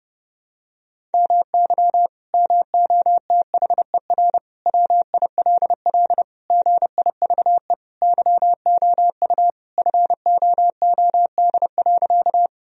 Morse code, heard as a keyed tone, about 30 wpm, 700 Hz. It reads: MY MOTHER WILL GIVE YOU FOOD.
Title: Sentences from Top 1000 Words - Review Part 1 - 30wpm